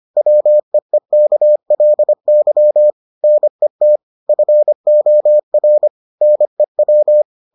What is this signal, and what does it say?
Morse code, heard as a keyed tone, about 25 wpm, 600 Hz. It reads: WEEKLY NET FOR NEW